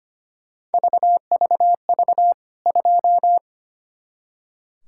Morse code, heard as a keyed tone, about 25 wpm, 700 Hz. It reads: VVV 2
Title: K1USN SST QSO Practice 25wpm